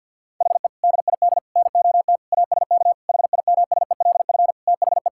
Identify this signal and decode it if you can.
Morse code, heard as a keyed tone, about 50 wpm, 700 Hz. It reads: HE DID NOT ASK HIMSELF THE
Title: Sentences from Top 1600 Words 50wpm